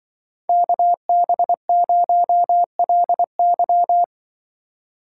Morse code, heard as a keyed tone, about 24 wpm, 700 Hz. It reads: KB0LY
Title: Speed Racing - USA 2x2 Call Signs 20wpm